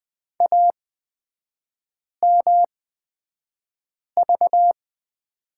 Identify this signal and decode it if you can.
Morse code, heard as a keyed tone, about 20 wpm, spaced as for 5 wpm, 700 Hz. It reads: AMV
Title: 224 Lesson 36e - BT - Three Characters 20wpm